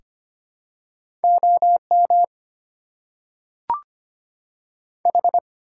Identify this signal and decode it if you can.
Morse code, heard as a keyed tone, about 25 wpm, 700 Hz. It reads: O M H
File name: Top 90 QSO Elements - 6x Character Spacing - 25wpm